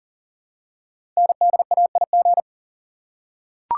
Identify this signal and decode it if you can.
Morse code, heard as a keyed tone, about 38 wpm, 690 Hz. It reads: NDAIG
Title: Code Groups - 5 Letters - 40wpm